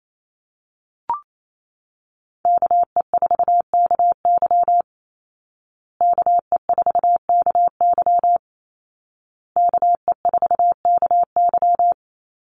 Morse code, heard as a keyed tone, about 28 wpm, 700 Hz. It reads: KE4KY KE4KY KE4KY
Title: Call Signs Repeated 2x - 28wpm